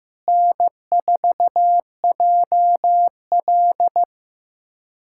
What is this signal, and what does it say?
Morse code, heard as a keyed tone, about 15 wpm, 700 Hz. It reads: N4JL
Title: USA Callsign DE Callsign - 4 Characters - 15wpm